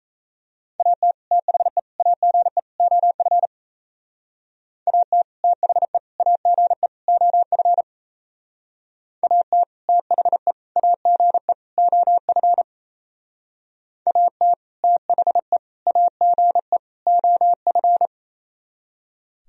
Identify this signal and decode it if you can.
Morse code, heard as a keyed tone, about 34 wpm, 700 Hz. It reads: AT THE AGE OF AT THE AGE OF AT THE AGE OF AT THE AGE OF
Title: Speed-Racing - Sets of 4 Words 28wpm